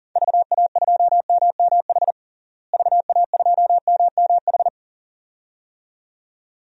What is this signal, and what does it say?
Morse code, heard as a keyed tone, about 40 wpm, 700 Hz. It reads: VA2MMH VA2MMH
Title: DXCC-100 Call Signs - 6 Characters - Repeated 2x - 40wpm